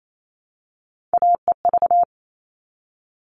Morse code, heard as a keyed tone, about 28 wpm, 700 Hz. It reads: AEV
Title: All Cut Numbers - 3-Digit 28wpm